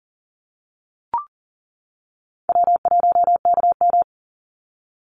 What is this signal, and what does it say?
Morse code, heard as a keyed tone, about 40 wpm, 700 Hz. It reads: W1KM
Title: Easy Call Signs - 40wpm